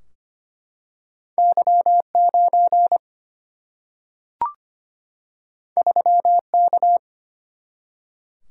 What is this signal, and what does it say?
Morse code, heard as a keyed tone, about 25 wpm, 700 Hz. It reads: Y 9 3 K
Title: Single-Letter Single-Number 25wpm